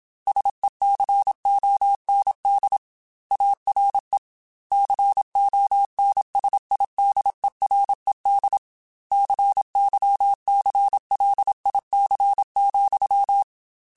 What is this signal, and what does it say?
Morse code, heard as a keyed tone, about 27 wpm, 800 Hz. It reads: SECOND ARE CONSIDERED CYCLIC,
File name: Technical Subjects / Technical Subjects-mp3s / XC @25